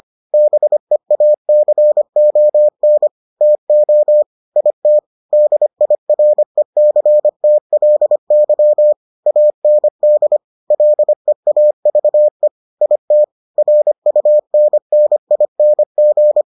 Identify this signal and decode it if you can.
Morse code, heard as a keyed tone, about 25 wpm, 600 Hz. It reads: BEACON TO IT DIRECTLY AND LEAVE IT RUNNING